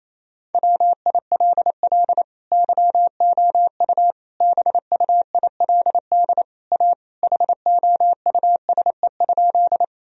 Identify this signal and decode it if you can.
Morse code, heard as a keyed tone, about 28 wpm, 700 Hz. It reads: WILL YOU BUILD A HOUSE?
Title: Sentences from Top 1000 Words - Review Part 1 - 28wpm